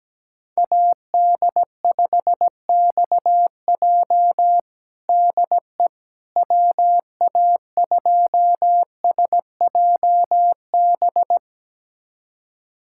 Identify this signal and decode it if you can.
Morse code, heard as a keyed tone, about 17 wpm, 700 Hz. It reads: AD5XJ DE WA2SJB
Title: USA Callsign DE Callsign 17wpm